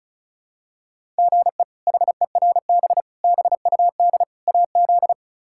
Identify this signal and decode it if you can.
Morse code, heard as a keyed tone, about 36 wpm, 700 Hz. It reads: GE HERB BUD AZ
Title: K1USN SST QSO Practice - 35wpm